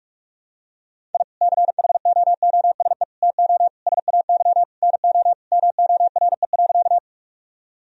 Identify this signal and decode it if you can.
Morse code, heard as a keyed tone, about 44 wpm, 700 Hz. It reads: I CHOOSE TO SAY NO MORE.
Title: Sentences from Top 1000 Words - Review Part 4 - 45wpm